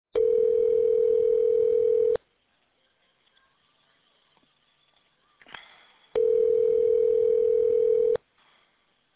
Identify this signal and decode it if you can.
Ringback tone (North America), 440 + 480 Hz.